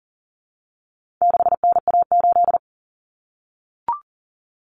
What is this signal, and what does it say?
Morse code, heard as a keyed tone, about 40 wpm, 700 Hz. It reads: 6NA8
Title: Common Cut Numbers - 4-Digit 40wpm